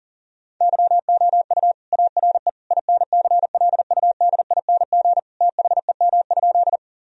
Morse code, heard as a keyed tone, about 40 wpm, 700 Hz. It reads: YOU ARE INCLUDING THEM?
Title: Sentences from Top 1000 Words - Review Part 4 - 40wpm